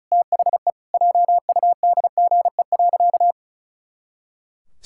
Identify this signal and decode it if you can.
Morse code, heard as a keyed tone, about 34 wpm, 700 Hz. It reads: THE JUDGE.
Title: Pangram Challenge v2 - 35wpm